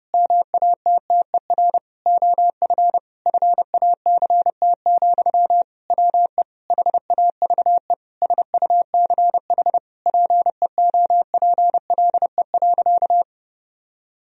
Morse code, heard as a keyed tone, about 30 wpm, 700 Hz. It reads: MATTER OF FACT, WE HAVE SUCH PEOPLE.
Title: Sentences from Top 1000 Words - Review Part 3 - 30wpm